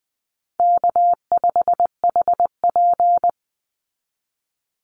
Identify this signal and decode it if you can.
Morse code, heard as a keyed tone, about 19 wpm, 700 Hz. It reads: K5HP
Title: Easy Call Signs Repeated 3x - 20wpm